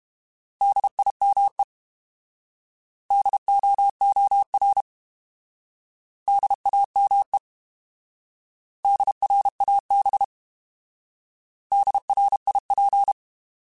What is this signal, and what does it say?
Morse code, heard as a keyed tone, about 32 wpm, 800 Hz. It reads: DIME DOOR DAME DRAB DRIP